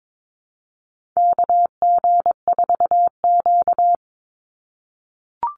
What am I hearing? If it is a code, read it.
Morse code, heard as a keyed tone, about 22 wpm, 700 Hz. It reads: KG4Q
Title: Call Signs - 22wpm